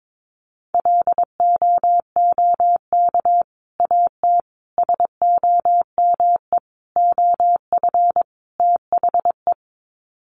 Morse code, heard as a keyed tone, about 22 wpm, 700 Hz. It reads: LOOK AT SOME OF THE
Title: Sets of 5 Words Encore 22wpm